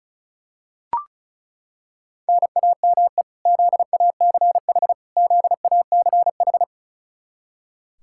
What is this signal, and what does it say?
Morse code, heard as a keyed tone, about 36 wpm, 700 Hz. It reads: NAME ZACH ZACH
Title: Practice QSO - Exchange 1 - Beginning - 35wpm